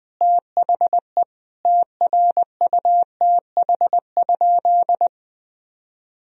Morse code, heard as a keyed tone, about 21 wpm, 700 Hz. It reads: THE TRUTH?
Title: Sentences from Top 1000 Words - Review Part 3 - 20wpm